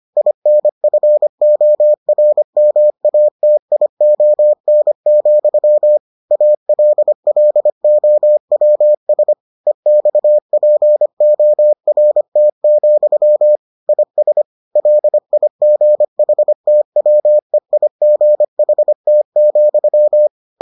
Morse code, heard as a keyed tone, about 25 wpm, 600 Hz. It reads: INFORMATION, ALLOWS EXPORT, IS LIGHTWEIGHT,